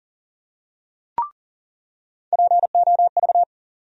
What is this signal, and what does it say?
Morse code, heard as a keyed tone, about 40 wpm, 700 Hz. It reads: POV
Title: USA Call Sign Postfixes - 3 Letters 40wpm